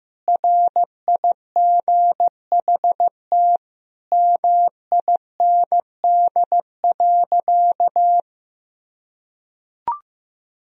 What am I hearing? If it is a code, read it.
Morse code, heard as a keyed tone, about 15 wpm, 700 Hz. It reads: RIGHT MIND.
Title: Sentences from Top 1000 Words - Review Part 1 - 15wpm